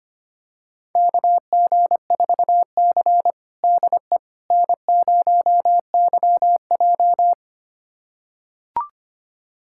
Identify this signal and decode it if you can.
Morse code, heard as a keyed tone, about 25 wpm, 700 Hz. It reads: KG4C DE N0YJ
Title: USA Callsign DE Callsign - 4 Characters - 25wpm